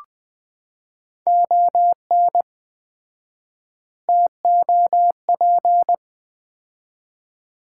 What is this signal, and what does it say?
Morse code, heard as a keyed tone, about 20 wpm, 700 Hz. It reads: ON TOP
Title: 4x Word-Spacing - Sets of 4 Words - 20wpm